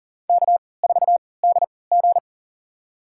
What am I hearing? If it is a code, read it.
Morse code, heard as a keyed tone, about 40 wpm, spaced as for 23 wpm, 700 Hz. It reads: K4DG
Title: USA Call Signs - 4 Characters - 3x Character Spacing - 40wpm